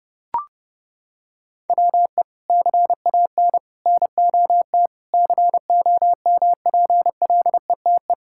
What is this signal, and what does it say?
Morse code, heard as a keyed tone, about 30 wpm, 700 Hz. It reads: WE CAN NOT COMPLETE
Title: Sentences from Top 1000 Words - Review Part 6 - 30wpm